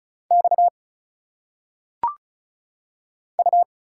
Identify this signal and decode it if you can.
Morse code, heard as a keyed tone, about 33 wpm, 700 Hz. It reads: X U
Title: Single Letters - Phonetic Pronunciation 35wpm